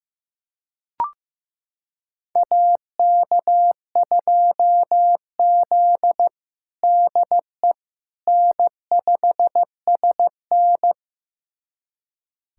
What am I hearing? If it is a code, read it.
Morse code, heard as a keyed tone, about 15 wpm, 700 Hz. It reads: AK2Z DE N5SN
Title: USA Callsign DE Callsign - 4 Characters - 15wpm